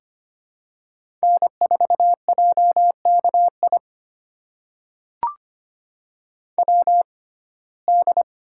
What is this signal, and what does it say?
Morse code, heard as a keyed tone, about 25 wpm, spaced as for 18 wpm, 700 Hz. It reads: N4JKI WD
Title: USA Call Signs - 6x Character Spacing - 25wpm